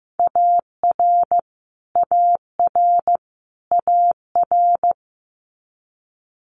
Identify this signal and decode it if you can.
Morse code, heard as a keyed tone, about 15 wpm, 700 Hz. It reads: AR AR AR
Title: Multi-letter Phonemes Repeated 3x 15wpm